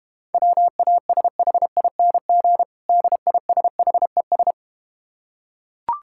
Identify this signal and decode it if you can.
Morse code, heard as a keyed tone, about 32 wpm, 700 Hz. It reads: WASHING DISHES